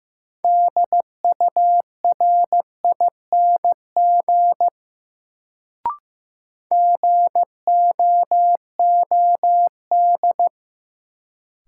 Morse code, heard as a keyed tone, about 15 wpm, 700 Hz. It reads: DURING GOOD
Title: Sets of 2 Words - New Word Drill - 15wpm